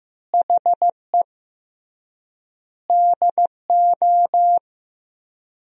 Morse code, heard as a keyed tone, about 16 wpm, 700 Hz. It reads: HE DO
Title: 3x Word-Spacing - Sentences from Top 100 Words - 15wpm